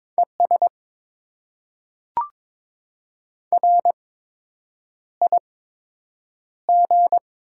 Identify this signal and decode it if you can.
Morse code, heard as a keyed tone, about 21 wpm, spaced as for 8 wpm, 700 Hz. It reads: ES RIG
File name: Top 20 QSO Elements - 8x Character Spacing - 22wpm